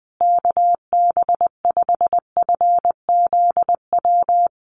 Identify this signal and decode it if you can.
Morse code, heard as a keyed tone, about 20 wpm, 700 Hz. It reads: KB5FZW
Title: USA Call Signs Repeated 3x 20wpm